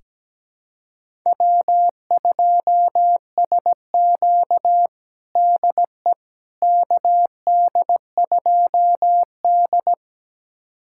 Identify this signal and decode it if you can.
Morse code, heard as a keyed tone, about 17 wpm, 700 Hz. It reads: W2SQ DE KD2D
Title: USA Callsign DE Callsign - 4 Characters - 17wpm